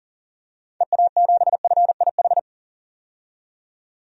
Morse code, heard as a keyed tone, about 40 wpm, 700 Hz. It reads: EA7FIH
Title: DXCC-100 Call Signs - 6 Characters - 40wpm